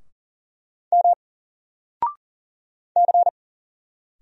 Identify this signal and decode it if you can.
Morse code, heard as a keyed tone, about 40 wpm, 700 Hz. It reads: M C
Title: Single Letter-Number 40wpm